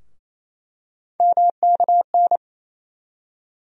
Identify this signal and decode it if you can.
Morse code, heard as a keyed tone, about 28 wpm, 700 Hz. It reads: MKN
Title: USA Call Sign Postfixes - 3 Letters 28wpm